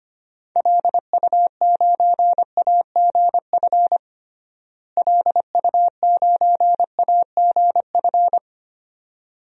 Morse code, heard as a keyed tone, about 25 wpm, 700 Hz. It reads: LU9AGF LU9AGF
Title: DXCC-100 Call Signs - 6 Characters - Repeated 2x - 25wpm